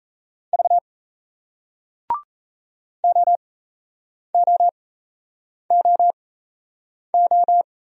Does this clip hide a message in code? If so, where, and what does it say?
Yes: Morse code from 0.5 to 7.6 s, heard as a keyed tone, about 38 wpm, spaced as for 8 wpm, 700 Hz. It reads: V OOOO